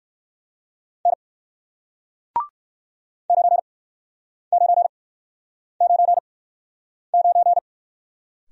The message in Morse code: N 9999

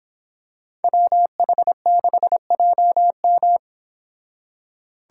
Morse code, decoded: WH6JM

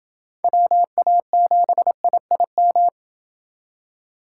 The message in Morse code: WA7IIM

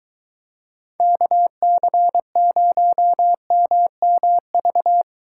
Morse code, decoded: KC0MMV

